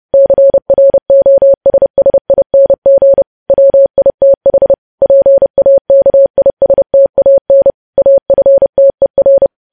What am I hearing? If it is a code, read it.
Morse code, heard as a keyed tone, about 30 wpm, 550 Hz. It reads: CROSSING WITH PAKISTAN AFTER
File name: podcast / News headlines-1x-30x30